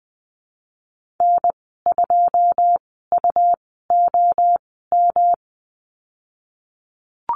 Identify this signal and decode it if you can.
Morse code, heard as a keyed tone, about 20 wpm, spaced as for 15 wpm, 700 Hz. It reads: N2UOM
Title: 161 Lesson 26g - Letter g - Call Signs 20wpm